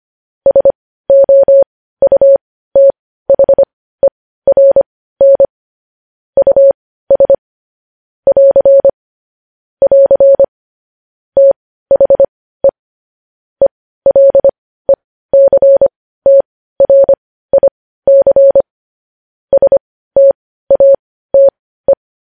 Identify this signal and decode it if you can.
Morse code, heard as a keyed tone, about 25 wpm, spaced as for 15 wpm, 550 Hz. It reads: SOUTHERN US <AR> <AR> THE ELECTRIC STATE
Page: news headlines in morse code at 15 wpm